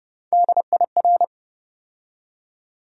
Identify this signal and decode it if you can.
Morse code, heard as a keyed tone, about 30 wpm, 700 Hz. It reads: DIR
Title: USA Call Sign Postfixes - 3 Letters 30wpm